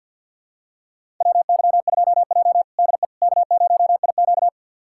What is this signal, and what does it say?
Morse code, heard as a keyed tone, about 52 wpm, 700 Hz. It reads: WY2J DE K0IQ
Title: USA Callsign DE Callsign - 4 Characters - 50wpm